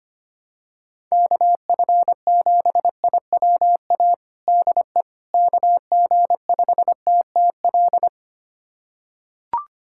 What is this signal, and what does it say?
Morse code, heard as a keyed tone, about 25 wpm, 700 Hz. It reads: KF7IWA DE KG5TTL
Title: USA Callsign DE Callsign 25wpm